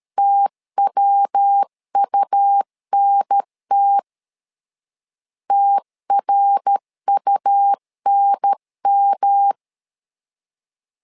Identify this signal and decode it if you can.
Morse code, heard as a keyed tone, about 13 wpm, 800 Hz. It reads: TWUNT TRUNM
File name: LEZIONI / 3 MWRO / 3 MWR 12wpm 1711